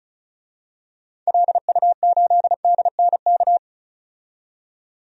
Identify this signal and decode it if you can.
Morse code, heard as a keyed tone, about 34 wpm, 700 Hz. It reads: LU8DNK
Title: DXCC-100 Call Signs - 6 Characters - 35wpm